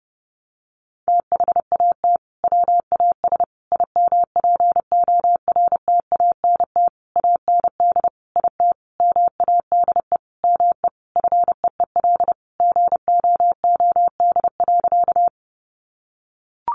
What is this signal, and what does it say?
Morse code, heard as a keyed tone, about 30 wpm, 700 Hz. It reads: THAT WAS IMPORTANT AND IT MADE ME FEEL GOOD.
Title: Sentences from Top 600 Words 30wpm